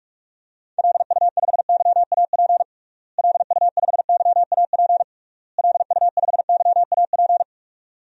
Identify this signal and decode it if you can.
Morse code, heard as a keyed tone, about 41 wpm, 700 Hz. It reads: LU5YAP LU5YAP LU5YAP
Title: DXCC-100 Call Signs - 6 Characters - Repeated 3x - 45wpm